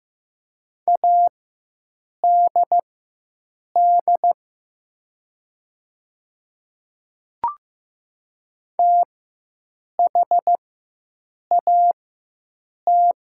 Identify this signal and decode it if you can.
Morse code, heard as a keyed tone, about 15 wpm, spaced as for 7 wpm, 700 Hz. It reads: ADD THAT
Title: 4x Character Spacing - Top 200 Words - 15wpm